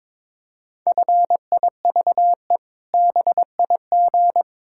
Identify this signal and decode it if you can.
Morse code, heard as a keyed tone, about 22 wpm, 700 Hz. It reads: FIVE BIG
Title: Pangram Challenge v2 - 22wpm